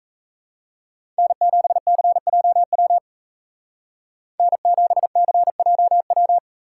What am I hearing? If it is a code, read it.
Morse code, heard as a keyed tone, about 36 wpm, 700 Hz. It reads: N7CJW N7CJW